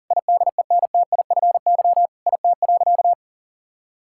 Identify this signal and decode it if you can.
Morse code, heard as a keyed tone, about 40 wpm, 690 Hz. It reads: IDENTIFY IT.
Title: Sentences from Top 1000 Words - Review Part 6 - 40wpm